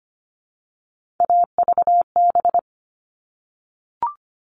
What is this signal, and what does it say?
Morse code, heard as a keyed tone, about 25 wpm, 700 Hz. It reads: AVB
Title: All Cut Numbers - 3-Digit 25wpm